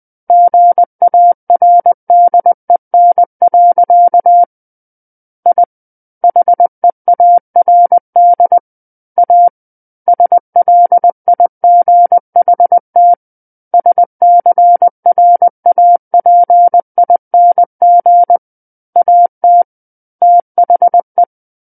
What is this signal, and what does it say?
Morse code, heard as a keyed tone, about 20 wpm, 700 Hz. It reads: GARDEN. I HEARD A SLIGHT SCRAPING AT THE
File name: 20 WPM